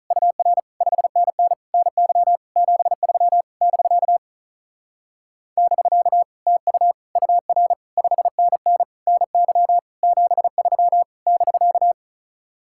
Morse code, heard as a keyed tone, about 45 wpm, 700 Hz. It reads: UR 5NN NY 73 <BK> <BK> TU UR 5NN NY 73 <BK>